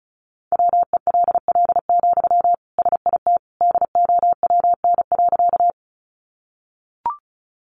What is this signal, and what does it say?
Morse code, heard as a keyed tone, about 34 wpm, 700 Hz. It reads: WELL, SIT DOWN.
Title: Sentences from Top 500 Words 35wpm